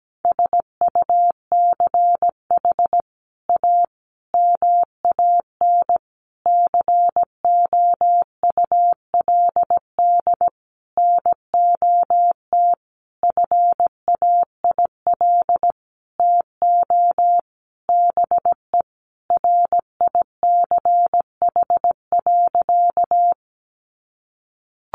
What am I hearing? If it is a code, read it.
Morse code, heard as a keyed tone, about 17 wpm, 700 Hz. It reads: SUCH A MAN COULD NOT FAIL TO BE RICH.
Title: Sentences from Top 1100 Words 17wpm